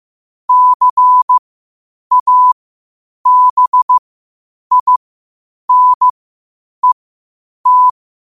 Morse code, heard as a keyed tone, about 15 wpm, spaced as for 8 wpm, 1000 Hz. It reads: CABINET
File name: qotdfiles / 8wpm